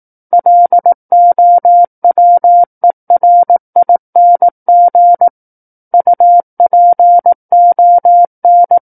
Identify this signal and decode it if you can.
Morse code, heard as a keyed tone, about 18 wpm, 700 Hz. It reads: LOWERING UPON
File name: 18 WPM